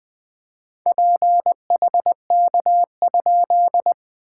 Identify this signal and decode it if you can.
Morse code, heard as a keyed tone, about 21 wpm, 690 Hz. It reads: PHK?
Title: SOTA QSO Practice 20wpm